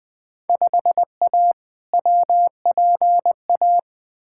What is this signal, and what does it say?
Morse code, heard as a keyed tone, about 20 wpm, 700 Hz. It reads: 5A WPA
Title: Field Day Exchanges Encore - 20wpm